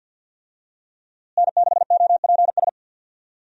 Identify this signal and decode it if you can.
Morse code, heard as a keyed tone, about 51 wpm, 700 Hz. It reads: N6OPS
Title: USA Call Signs - 5 Characters - Repeated 2x - 50wpm